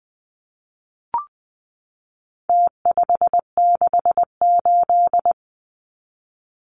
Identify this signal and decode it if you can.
Morse code, heard as a keyed tone, about 20 wpm, 700 Hz. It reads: T568
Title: Numbers - Contest Style - 4-Digit 20wpm